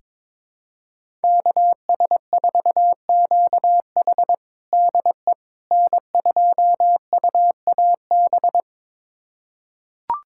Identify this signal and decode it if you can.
Morse code, heard as a keyed tone, about 22 wpm, 700 Hz. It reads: KS4QH DE N2UAB
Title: USA Callsign DE Callsign - 5 Characters - 22wpm